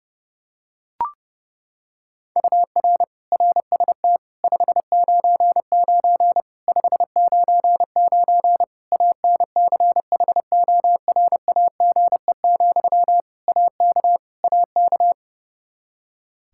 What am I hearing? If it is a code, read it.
Morse code, heard as a keyed tone, about 30 wpm, 700 Hz. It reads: UR RST 599 599 ANCHORAGE, AK AK
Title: Practice QSO - Exchange 1 - Beginning - 30wpm